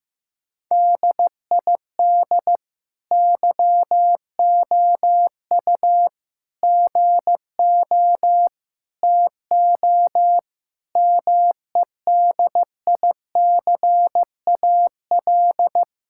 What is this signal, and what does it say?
Morse code, heard as a keyed tone, about 15 wpm, 700 Hz. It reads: DID YOU GO TO MEDICAL
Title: Sentences from Top 1000 Words - Review Part 5 - 15wpm